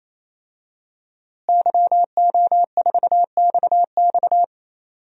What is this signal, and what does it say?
Morse code, heard as a keyed tone, about 28 wpm, 700 Hz. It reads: YO4XX
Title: DXCC-100 Call Signs - 5 Characters - Repeated 3x - 28wpm